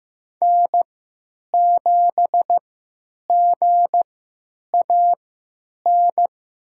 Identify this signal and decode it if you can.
Morse code, heard as a keyed tone, about 15 wpm, spaced as for 8 wpm, 700 Hz. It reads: N7GAN
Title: USA Call Signs - 5 Characters - 3x Character Spacing - 15wpm